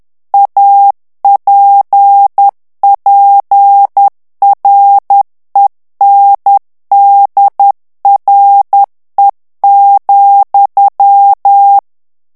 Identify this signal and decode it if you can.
Morse code, heard as a keyed tone, about 10 wpm, 800 Hz. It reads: APPRENDRE,
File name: FRANCAIS TEXTE CLAIR 2 VITESSE 10 MOTS MINUTES